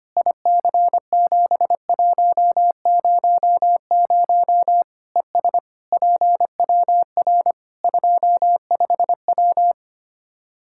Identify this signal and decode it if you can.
Morse code, heard as a keyed tone, about 25 wpm, 700 Hz. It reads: IC7100 ES PWR 25W
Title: K1USN SST QSO Practice 25wpm